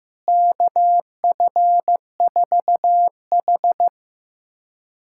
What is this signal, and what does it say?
Morse code, heard as a keyed tone, about 16 wpm, 700 Hz. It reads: KF4H
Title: USA Callsign DE Callsign - 4 Characters - 15wpm